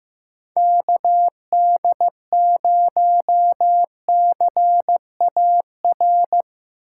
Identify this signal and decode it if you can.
Morse code, heard as a keyed tone, about 15 wpm, 700 Hz. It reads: KD0CAR
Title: USA Call Signs - 6 Characters - 2x Character Spacing - 15wpm